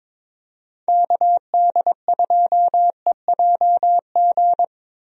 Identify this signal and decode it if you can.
Morse code, heard as a keyed tone, about 22 wpm, 700 Hz. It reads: KD2EJG
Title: USA Call Signs - 6 Characters - Repeated 3x - 22wpm